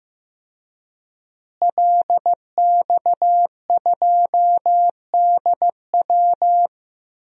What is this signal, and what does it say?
Morse code, heard as a keyed tone, about 16 wpm, 700 Hz. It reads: LX2DW
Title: DXCC-100 Call Signs - 5 Characters - Repeated 2x - 15wpm